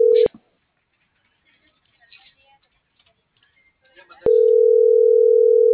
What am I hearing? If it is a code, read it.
Ringback tone (North America), 440 + 480 Hz.
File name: CENTRAL AGUILERA / out ram.lmorena.2025 - .68864